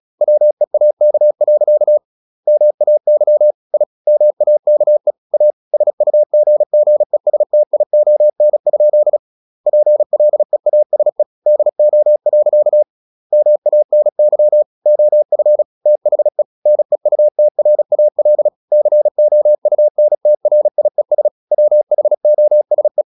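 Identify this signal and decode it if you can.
Morse code, heard as a keyed tone, about 36 wpm, 600 Hz. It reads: WEAK. MAY I MAKE A SUGGESTION? PLEASE DO. MANY OF THE NEUTRAL COUNTRIES WHOSE
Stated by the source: Morse practice at 36 wpm, text of Ken Follett's novel Never, from CW med Gnister.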